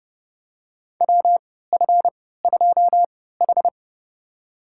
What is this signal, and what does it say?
Morse code, heard as a keyed tone, about 32 wpm, spaced as for 17 wpm, 700 Hz. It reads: WF2H
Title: USA Call Signs - 4 Characters - 3x Character Spacing - 30wpm